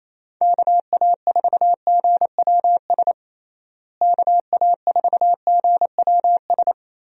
Morse code, heard as a keyed tone, about 28 wpm, 700 Hz. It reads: KA4GWS KA4GWS
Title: USA Call Signs - 6 Characters - Repeated 2x - 28wpm